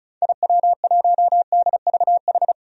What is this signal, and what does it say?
Morse code, heard as a keyed tone, about 34 wpm, 700 Hz. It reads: IW1DVH
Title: DXCC-100 Call Signs - 6 Characters - 35wpm